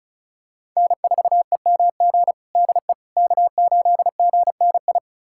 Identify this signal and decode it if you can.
Morse code, heard as a keyed tone, about 36 wpm, 700 Hz. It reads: N4EMG DE K8GNI
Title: USA Callsign DE Callsign - 5 Characters - 35wpm